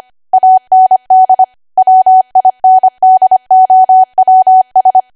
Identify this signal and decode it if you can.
Morse code, heard as a keyed tone, about 25 wpm, 750 Hz. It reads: AND WINDOWS
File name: cwa / W1AW-Level3 / 25WPM